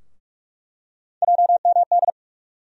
Morse code, heard as a keyed tone, about 47 wpm, 700 Hz. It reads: JMD
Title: USA Call Sign Postfixes - 3 Letters 45wpm